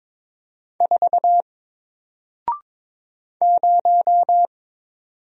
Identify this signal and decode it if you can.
Morse code, heard as a keyed tone, about 22 wpm, 700 Hz. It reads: 4 0
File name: Single-Letter-Number-22wpm